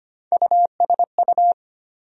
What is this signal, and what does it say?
Morse code, heard as a keyed tone, about 25 wpm, 700 Hz. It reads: USU
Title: USA Call Sign Postfixes - 3 Letters 25wpm